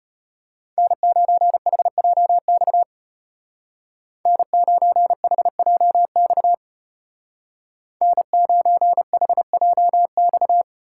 Morse code, heard as a keyed tone, about 33 wpm, 700 Hz. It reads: N9HJX N9HJX N9HJX